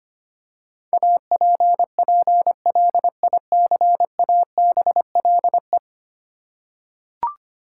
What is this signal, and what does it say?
Morse code, heard as a keyed tone, about 25 wpm, 700 Hz. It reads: APPLICABLE
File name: practice / 25wpm / Ten Letter Words - 25wpm